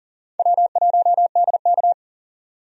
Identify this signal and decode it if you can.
Morse code, heard as a keyed tone, about 40 wpm, 700 Hz. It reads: W1DK
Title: USA Call Signs - 4 Characters - 40wpm